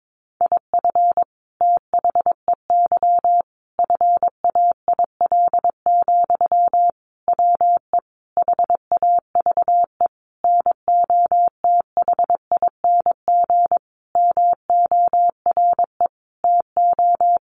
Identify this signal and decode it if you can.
Morse code, heard as a keyed tone, about 22 wpm, 700 Hz. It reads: IF THEY FAIL, WE HAVE NOTHING MORE TO
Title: Sentences from Top 1100 Words 22wpm